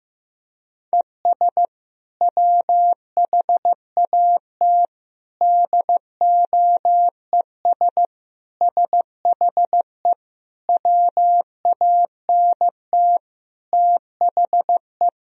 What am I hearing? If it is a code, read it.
Morse code, heard as a keyed tone, about 15 wpm, 700 Hz. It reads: ES WHAT DOES SHE WANT THE
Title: Sentences from Top 400 Words and Abbreviations 15wpm